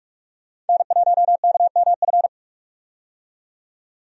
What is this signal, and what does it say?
Morse code, heard as a keyed tone, about 46 wpm, 700 Hz. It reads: N1KMF
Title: USA Call Signs - 5 Characters - 45wpm